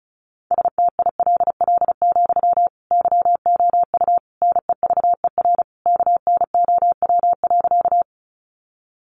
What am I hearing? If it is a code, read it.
Morse code, heard as a keyed tone, about 34 wpm, 700 Hz. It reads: STILL, YOU NEVER KNOW.
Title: Sentences from Top 300 Words 35wpm